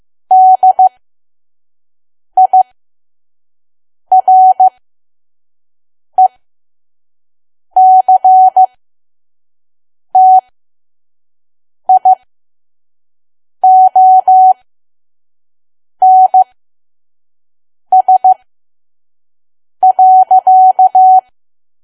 Morse code, heard as a keyed tone, about 15 wpm, spaced as for 5 wpm, 750 Hz. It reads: DIRECTIONS.
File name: Archive / 5 WPM files / PM